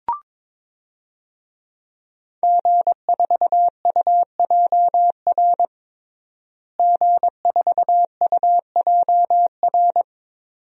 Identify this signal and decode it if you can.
Morse code, heard as a keyed tone, about 22 wpm, 700 Hz. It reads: G4UJR G4UJR
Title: DXCC-100 Call Signs - 5 Characters - Repeated 2x - 22wpm